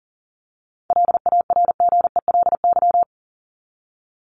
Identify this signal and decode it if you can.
Morse code, heard as a keyed tone, about 40 wpm, 700 Hz. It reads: LARGELY
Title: LY Suffix - Top 100 Words with Base Words 40wpm